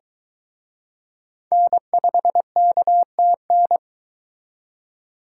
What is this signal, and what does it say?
Morse code, heard as a keyed tone, about 23 wpm, 700 Hz. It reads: N5KTN